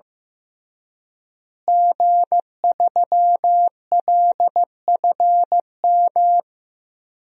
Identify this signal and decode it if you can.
Morse code, heard as a keyed tone, about 15 wpm, 700 Hz. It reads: G3LFM
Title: DXCC-100 Call Signs - 5 Characters - Repeated 3x - 15wpm